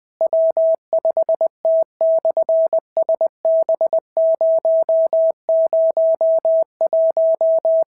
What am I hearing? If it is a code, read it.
Morse code, heard as a keyed tone, about 20 wpm, 650 Hz. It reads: W5T/SB001
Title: SOTA QSO Practice 20wpm